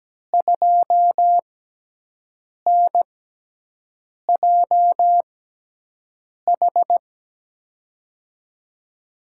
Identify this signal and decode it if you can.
Morse code, heard as a keyed tone, about 17 wpm, spaced as for 6 wpm, 700 Hz. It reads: 2NJH
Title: USA Call Signs - 5 Characters - 6x Character Spacing - 17wpm